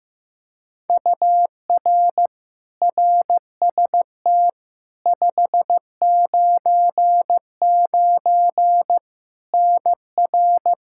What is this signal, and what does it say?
Morse code, heard as a keyed tone, about 15 wpm, 700 Hz. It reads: UR RST 599 NR
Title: K1USN SST QSO Practice 15wpm